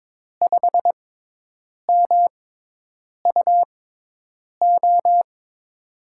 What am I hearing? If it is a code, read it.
Morse code, heard as a keyed tone, about 22 wpm, spaced as for 8 wpm, 700 Hz. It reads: 5MUO
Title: USA Call Signs - 6 Characters - 6x Character Spacing - 22wpm